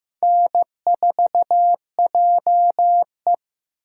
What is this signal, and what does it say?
Morse code, heard as a keyed tone, about 18 wpm, 700 Hz. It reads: N4JE
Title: Speed Racing - USA 1x2 Call Signs 15wpm